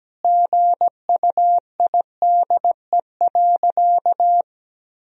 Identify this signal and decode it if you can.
Morse code, heard as a keyed tone, about 17 wpm, 700 Hz. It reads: GUIDE.
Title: Sentences from Top 1000 Words - Review Part 3 - 17wpm